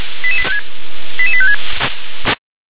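Five-tone selective call (ZVEI), heard as quick runs of five tones, tones of about 70 ms. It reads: 88856
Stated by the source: ZVEI five-tone selective call received by SDR.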